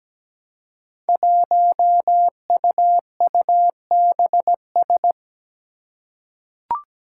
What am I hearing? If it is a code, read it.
Morse code, heard as a keyed tone, about 17 wpm, 700 Hz. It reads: 1UUBS
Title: Five Challenging Letter-Number Combinations 17wpm